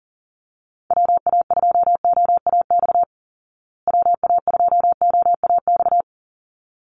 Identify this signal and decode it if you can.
Morse code, heard as a keyed tone, about 40 wpm, 700 Hz. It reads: WA2OAX WA2OAX
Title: Call Signs Repeated 3x - 40wpm